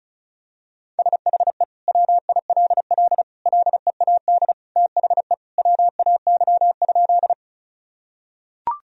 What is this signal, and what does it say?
Morse code, heard as a keyed tone, about 34 wpm, 700 Hz. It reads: SHE WILL LEAD THE WAY?
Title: Sentences from Top 1000 Words - Review Part 2 - 35wpm